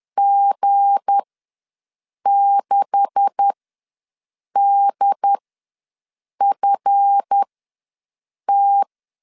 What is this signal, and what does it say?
Morse code, heard as a keyed tone, about 10 wpm, spaced as for 6 wpm, 800 Hz. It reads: G6DFT
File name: LEZIONI / 10 SIGNS / 10B INT ws 10wpm 4134